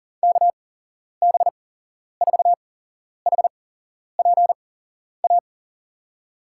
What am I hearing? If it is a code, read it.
Morse code, heard as a keyed tone, about 43 wpm, spaced as for 11 wpm, 700 Hz. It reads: KB4HPA
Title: USA Call Signs - 8x Character Spacing - 40wpm